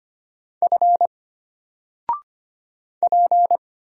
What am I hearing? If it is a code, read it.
Morse code, heard as a keyed tone, about 25 wpm, 700 Hz. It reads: F P
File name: Single-Letter-Number-25wpm